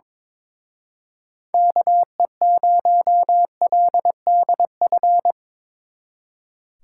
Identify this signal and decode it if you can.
Morse code, heard as a keyed tone, about 23 wpm, 700 Hz. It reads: KE0LDF
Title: Speed Racing - USA 2x3 Call Signs 22wpm